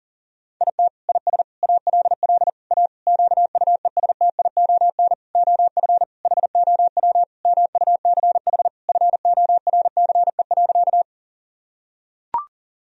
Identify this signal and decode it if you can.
Morse code, heard as a keyed tone, about 40 wpm, 700 Hz. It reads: IT IS ALL A QUESTION OF HOW MUCH FORCE.
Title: Sentences from Top 1000 Words - Review Part 3 - 40wpm